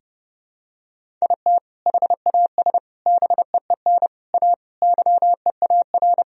Morse code, heard as a keyed tone, about 30 wpm, 700 Hz. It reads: IT HAS BEEN A YEAR
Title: Sentences from Top 1000 Words - Review Part 1 - 30wpm